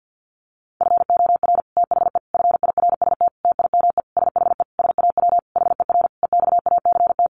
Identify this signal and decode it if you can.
Morse code, heard as a keyed tone, about 50 wpm, 700 Hz. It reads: FOR THE FIRST TIME SHE SAW HER EXACT
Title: Sentences from Top 1300 Words 50wpm